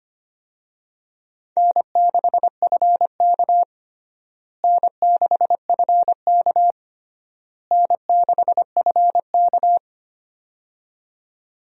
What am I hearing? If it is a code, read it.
Morse code, heard as a keyed tone, about 25 wpm, 700 Hz. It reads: N6FK N6FK N6FK
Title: USA Call Signs - 4 Characters - Repeated 3x - 25wpm